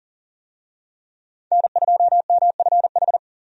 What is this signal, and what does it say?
Morse code, heard as a keyed tone, about 40 wpm, 700 Hz. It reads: N2MFH
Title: USA Call Signs - 5 Characters - Repeated 2x - 40wpm